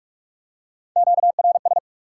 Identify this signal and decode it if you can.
Morse code, heard as a keyed tone, about 44 wpm, 700 Hz. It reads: QRS
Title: Top 90 QSO Elements 45wpm